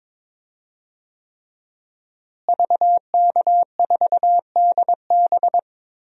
Morse code, heard as a keyed tone, about 21 wpm, 700 Hz. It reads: VK4DB
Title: DXCC-100 Call Signs - 5 Characters - Repeated 3x - 22wpm